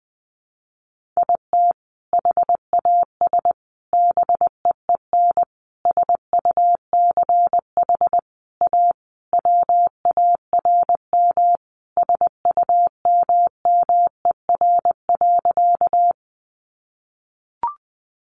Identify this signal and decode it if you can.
Morse code, heard as a keyed tone, about 20 wpm, 700 Hz. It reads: IT HAS BEEN SUCH A WARM SUMMER.
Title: Sentences from Top 1100 Words 20wpm